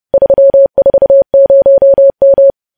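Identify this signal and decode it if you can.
Morse code, heard as a keyed tone, about 30 wpm, 550 Hz. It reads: 340M
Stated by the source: Morse news headlines sent at 30 wpm, opening with vvv.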